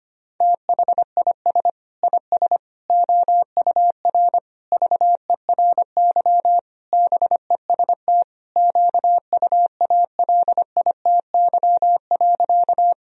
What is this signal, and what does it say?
Morse code, heard as a keyed tone, about 25 wpm, 700 Hz. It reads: THIS IS OUR VERY BEST QUALITY.
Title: Sentences from Top 1000 Words - Review Part 4 - 25wpm